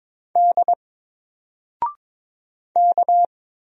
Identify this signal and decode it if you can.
Morse code, heard as a keyed tone, about 22 wpm, 700 Hz. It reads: D K
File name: Single-Letter-Number-22wpm